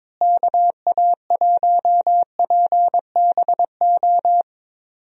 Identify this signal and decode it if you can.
Morse code, heard as a keyed tone, about 22 wpm, 700 Hz. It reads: KA1PBO
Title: USA Call Signs - 6 Characters - Repeated 2x - 22wpm